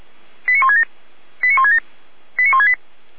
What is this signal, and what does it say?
Five-tone selective call (ZVEI), heard as quick runs of five tones, tones of about 70 ms. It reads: 89167 89167 89168